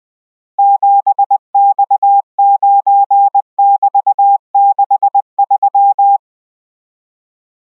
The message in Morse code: 7X9<BT>63